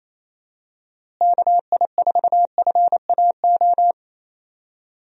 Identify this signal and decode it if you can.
Morse code, heard as a keyed tone, about 28 wpm, 700 Hz. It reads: KI4FAO
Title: USA Call Signs - 6 Characters - Repeated 3x - 28wpm